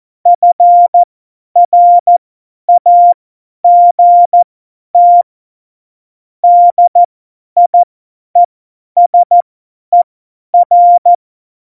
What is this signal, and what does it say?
Morse code, heard as a keyed tone, about 14 wpm, spaced as for 10 wpm, 700 Hz. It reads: FRAGT DIESER
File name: Morsekurs / Witze / Witz 05. Dezember 2017, 50 F70